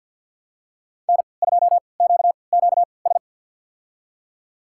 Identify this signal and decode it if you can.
Morse code, heard as a keyed tone, about 52 wpm, spaced as for 29 wpm, 690 Hz. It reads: N2QQS
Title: USA Call Signs - 3x Character Spacing - 50wpm